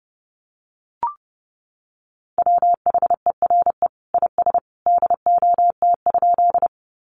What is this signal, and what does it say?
Morse code, heard as a keyed tone, about 30 wpm, 700 Hz. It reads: WHERE IS DOT?
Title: Sentences from Top 1500 Words 30wpm